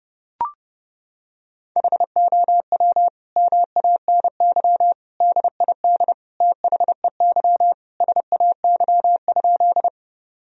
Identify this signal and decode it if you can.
Morse code, heard as a keyed tone, about 30 wpm, 700 Hz. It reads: HOW MANY DID THEY SAY?
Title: Sentences from Top 1000 Words - Review Part 1 - 30wpm